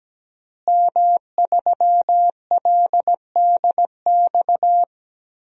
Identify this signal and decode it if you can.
Morse code, heard as a keyed tone, about 17 wpm, 700 Hz. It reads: M3LDX
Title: DXCC-100 Call Signs - 5 Characters - Repeated 2x - 17wpm